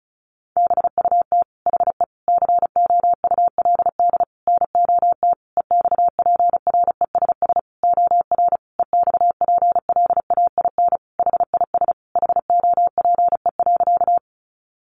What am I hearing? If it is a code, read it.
Morse code, heard as a keyed tone, about 34 wpm, 700 Hz. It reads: BUT HE COULD NOT EXPRESS OR EXPLAIN HIS HOPE.
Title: Sentences from Top 1100 Words 35wpm